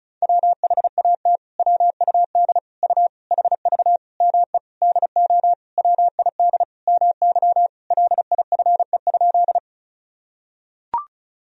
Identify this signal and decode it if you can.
Morse code, heard as a keyed tone, about 36 wpm, 700 Hz. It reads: WHAT WUD U HV ME DO WID MY LIFE?
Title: Sentences from Top 300 Words and Abbreviations 35wpm